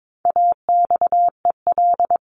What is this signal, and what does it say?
Morse code, heard as a keyed tone, about 22 wpm, 700 Hz. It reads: AXEL
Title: CWT Exchange 22wpm